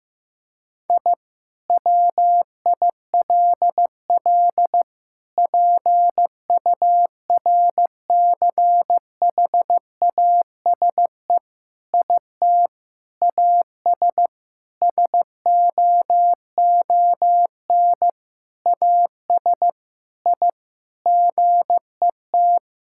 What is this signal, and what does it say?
Morse code, heard as a keyed tone, about 15 wpm, 700 Hz. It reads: I WILL PURCHASE IT AS SOON AS I GET
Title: Sentences from Top 1000 Words - Review Part 5 - 15wpm